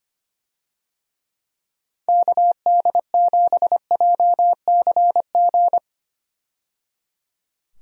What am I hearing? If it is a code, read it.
Morse code, heard as a keyed tone, about 25 wpm, 700 Hz. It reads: KD7JCG
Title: USA Call Signs - 6 Characters - 25wpm